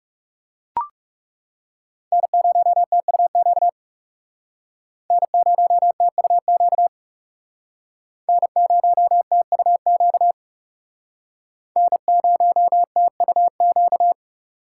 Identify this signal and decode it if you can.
Morse code, heard as a keyed tone, about 41 wpm, 700 Hz. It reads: N0TUQ N0TUQ N0TUQ N0TUQ